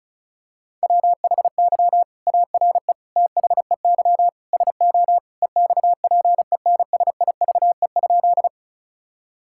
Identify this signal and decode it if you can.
Morse code, heard as a keyed tone, about 36 wpm, 700 Hz. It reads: WHY ARE THEY SO EXPENSIVE?
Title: Sentences from Top 1000 Words - Review Part 6 - 35wpm